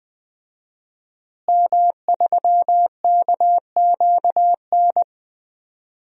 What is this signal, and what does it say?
Morse code, heard as a keyed tone, about 20 wpm, 700 Hz. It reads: M3KQN
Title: DXCC-100 Call Signs - 5 Characters - Repeated 3x - 20wpm